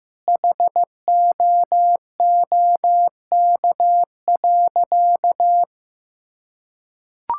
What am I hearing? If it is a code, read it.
Morse code, heard as a keyed tone, about 15 wpm, 700 Hz. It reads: HOOK.
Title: Sentences from Top 1600 Words 15wpm